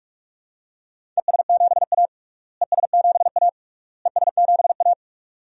Morse code, heard as a keyed tone, about 46 wpm, 700 Hz. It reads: ES7A ES7A ES7A
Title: DXCC-100 Call Signs - 4 Characters - Repeated 3x - 45wpm